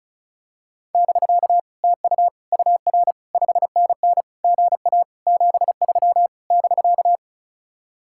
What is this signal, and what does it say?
Morse code, heard as a keyed tone, about 34 wpm, 700 Hz. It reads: <BK> TU UR 5NN GA 73 <BK>
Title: Speed-Racing Common QSO Phrases Encore - 35wpm